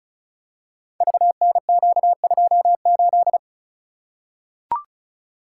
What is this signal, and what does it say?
Morse code, heard as a keyed tone, about 36 wpm, 690 Hz. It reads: VNQ28
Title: Code Groups - 5 Letter:Numbers - 35wpm